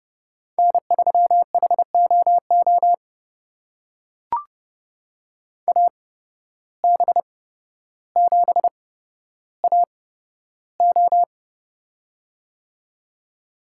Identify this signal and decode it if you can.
Morse code, heard as a keyed tone, about 30 wpm, spaced as for 13 wpm, 700 Hz. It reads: N3HOO AB7AO
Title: USA Call Signs - 5 Characters - 8x Character Spacing - 30wpm